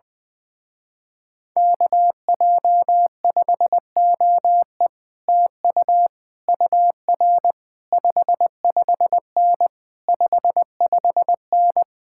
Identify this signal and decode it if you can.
Morse code, heard as a keyed tone, about 19 wpm, 700 Hz. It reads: KJ5OE TU UR 55N 55N
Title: Speed-Racing Common QSO Phrases - 20wpm